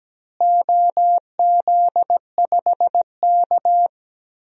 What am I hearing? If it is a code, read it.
Morse code, heard as a keyed tone, about 17 wpm, 700 Hz. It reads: OZ5K
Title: DXCC-100 Call Signs - 4 Characters - Repeated 2x - 17wpm